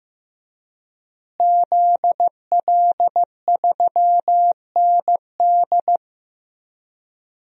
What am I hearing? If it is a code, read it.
Morse code, heard as a keyed tone, about 15 wpm, 700 Hz. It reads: ZL3ND
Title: DXCC-100 Call Signs - 5 Characters - Repeated 3x - 15wpm